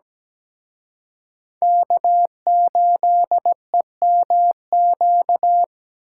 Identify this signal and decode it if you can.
Morse code, heard as a keyed tone, about 20 wpm, 700 Hz. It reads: K8EMQ
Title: Speed Racing - USA 1x3 Call Signs 17wpm